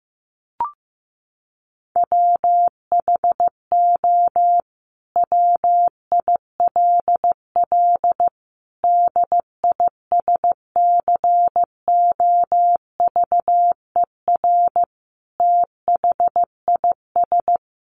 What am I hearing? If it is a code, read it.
Morse code, heard as a keyed tone, about 15 wpm, 700 Hz. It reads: WHO WILL DISCOVER THIS
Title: Sentences from Top 1100 Words 15wpm